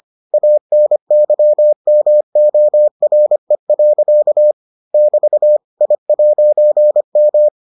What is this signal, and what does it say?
Morse code, heard as a keyed tone, about 25 wpm, 600 Hz. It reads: ANYMORE. <BT> I'M